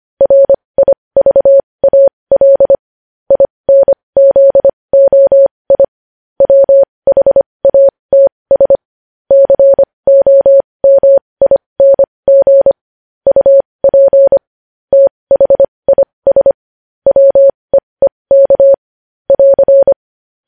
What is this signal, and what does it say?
Morse code, heard as a keyed tone, about 25 wpm, spaced as for 20 wpm, 550 Hz. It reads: RIVAL INZOI WHATS COMING UP THIS WEEK <AR>
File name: podcast / News headlines-1x-25x20